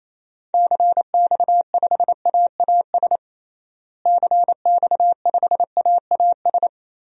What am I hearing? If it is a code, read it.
Morse code, heard as a keyed tone, about 28 wpm, 700 Hz. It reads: CX5AAS CX5AAS
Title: DXCC-100 Call Signs - 6 Characters - Repeated 3x - 28wpm